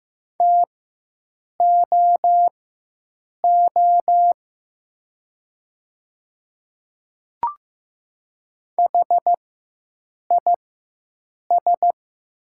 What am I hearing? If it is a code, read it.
Morse code, heard as a keyed tone, about 15 wpm, spaced as for 7 wpm, 700 Hz. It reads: TOO HIS
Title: 4x Character Spacing - Top 200 Words - 15wpm